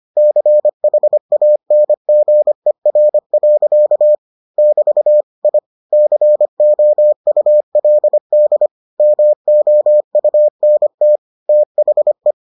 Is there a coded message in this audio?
Yes: Morse code from 0.2 to 12.3 s, heard as a keyed tone, about 25 wpm, 600 Hz. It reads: CHANGER. <BT> I COULD MOUNT THE